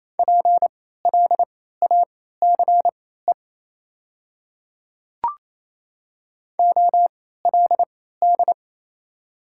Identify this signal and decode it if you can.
Morse code, heard as a keyed tone, about 28 wpm, spaced as for 16 wpm, 700 Hz. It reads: PLACE OLD
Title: 3x Character Spacing - Top 200 Words - 28wpm